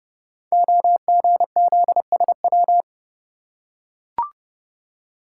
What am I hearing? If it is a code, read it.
Morse code, heard as a keyed tone, about 30 wpm, 700 Hz. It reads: OGZSW
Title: Code Groups - 5 Letter:Numbers - 30wpm